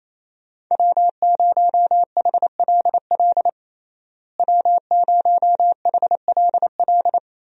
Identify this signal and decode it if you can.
Morse code, heard as a keyed tone, about 27 wpm, 700 Hz. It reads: W0HLL W0HLL
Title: USA Call Signs - 5 Characters - Repeated 2x - 28wpm